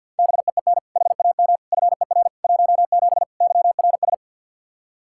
Morse code, heard as a keyed tone, about 51 wpm, 700 Hz. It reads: BEEN HAM FER 17 YRS
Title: K1USN SST QSO Practice 50wpm